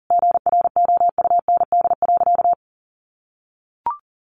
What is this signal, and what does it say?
Morse code, heard as a keyed tone, about 40 wpm, 700 Hz. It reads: GROUND.
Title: Easy Phrases 40wpm